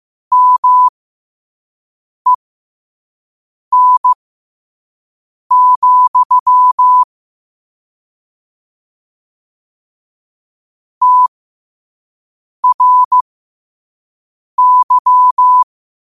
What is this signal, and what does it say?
Morse code, heard as a keyed tone, about 15 wpm, spaced as for 5 wpm, 1000 Hz. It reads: MEN, TRY
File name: qotdfiles / 5wpm